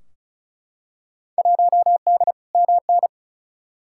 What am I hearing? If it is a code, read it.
Morse code, heard as a keyed tone, about 36 wpm, 700 Hz. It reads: 1D MN
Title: Field Day Exchanges - 35wpm